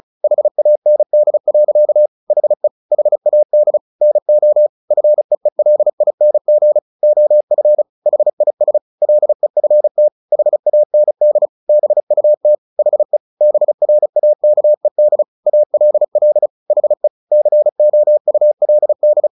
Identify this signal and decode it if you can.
Morse code, heard as a keyed tone, about 34 wpm, 600 Hz. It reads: HAND. HE HAD NO FEELING OF HIS LEFT HAND BUT HE BRAKED ALL HE COULD
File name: the old man and the sea 2 35wpm